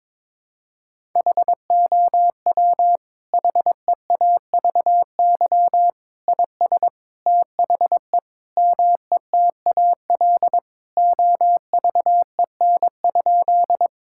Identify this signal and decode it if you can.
Morse code, heard as a keyed tone, about 22 wpm, 700 Hz. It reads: HOW HEAVY IS THE METAL OVEN?
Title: Sentences from Top 1000 Words - Review Part 6 - 22wpm